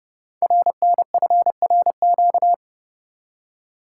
Morse code, heard as a keyed tone, about 30 wpm, 700 Hz. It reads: RNFRQ
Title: Code Groups - 5 Letters - 30wpm